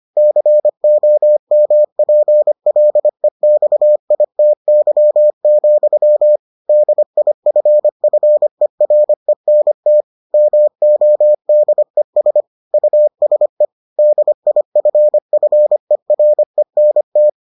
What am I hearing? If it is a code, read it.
Morse code, heard as a keyed tone, about 25 wpm, 600 Hz. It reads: COMPLEXITY, DIFFERENT MODES USE DIFFERENT